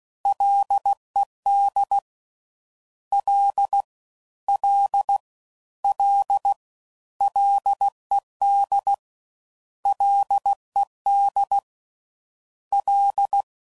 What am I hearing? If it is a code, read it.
Morse code, heard as a keyed tone, about 16 wpm, 800 Hz. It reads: LED L L L LED LED L
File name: Prefixes / FLs & THs / Single Fs Ls